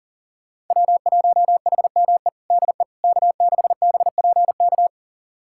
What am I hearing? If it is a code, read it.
Morse code, heard as a keyed tone, about 40 wpm, 700 Hz. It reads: W1HME DE K6BPK
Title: USA Callsign DE Callsign - 5 Characters - 40wpm